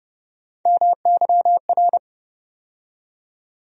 Morse code, heard as a keyed tone, about 31 wpm, 700 Hz. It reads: MYR
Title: USA Call Sign Postfixes - 3 Letters 30wpm